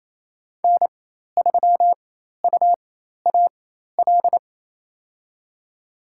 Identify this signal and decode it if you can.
Morse code, heard as a keyed tone, about 28 wpm, spaced as for 13 wpm, 700 Hz. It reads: N3UAL
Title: USA Call Signs - 5 Characters - 4x Character Spacing - 28wpm